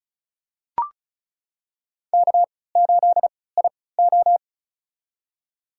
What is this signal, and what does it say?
Morse code, heard as a keyed tone, about 34 wpm, spaced as for 20 wpm, 700 Hz. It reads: K8IO
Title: USA Call Signs - 4 Characters - 3x Character Spacing - 35wpm